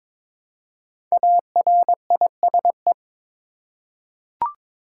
Morse code, heard as a keyed tone, about 22 wpm, 700 Hz. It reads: ARISE
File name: 5 Letter Words - First 3 Letters Given - 22wpm